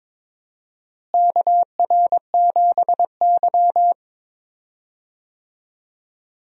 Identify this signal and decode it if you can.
Morse code, heard as a keyed tone, about 22 wpm, 700 Hz. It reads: KR7Y
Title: USA Call Signs - 4 Characters - Repeated 3x - 22wpm